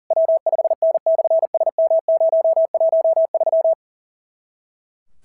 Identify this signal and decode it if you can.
Morse code, heard as a keyed tone, about 37 wpm, 660 Hz. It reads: W5N/SM013
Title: SOTA QSO Practice 40wpm